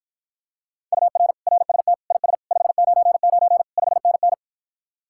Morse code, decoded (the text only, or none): UR RST IS 599 5NN